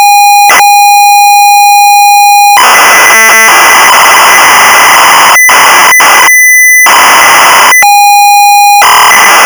ZX Spectrum tape loading.